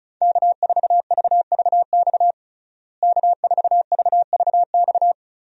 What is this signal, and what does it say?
Morse code, heard as a keyed tone, about 36 wpm, 700 Hz. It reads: K4VVX K4VVX
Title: USA Call Signs - 5 Characters - Repeated 3x - 35wpm